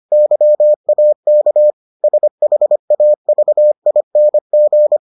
Morse code, heard as a keyed tone, about 25 wpm, 600 Hz. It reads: YAK SHAVING